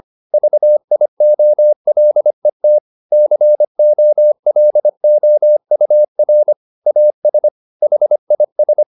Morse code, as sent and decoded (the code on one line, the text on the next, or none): ...- .. --- .-.. - / -.-. --- .-.. --- ..- .-. / .- ... / .... .. ...
VIOLET COLOUR AS HIS